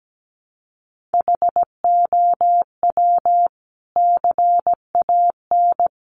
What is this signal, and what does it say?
Morse code, heard as a keyed tone, about 17 wpm, 700 Hz. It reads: HOW CAN
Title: Sentences from Top 100 Words 17wpm